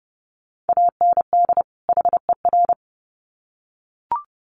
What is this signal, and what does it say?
Morse code, heard as a keyed tone, about 30 wpm, 700 Hz. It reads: AND HER
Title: Sets of 2 Words 30wpm